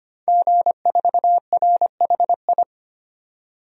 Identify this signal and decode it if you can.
Morse code, heard as a keyed tone, about 25 wpm, 700 Hz. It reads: G4RHI
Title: DXCC-100 Call Signs - 5 Characters - 25wpm